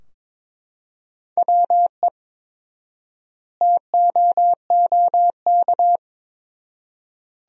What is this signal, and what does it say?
Morse code, heard as a keyed tone, about 22 wpm, 700 Hz. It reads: WE TOOK
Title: 4x Word-Spacing - Sentences from Top 400 Words - 22wpm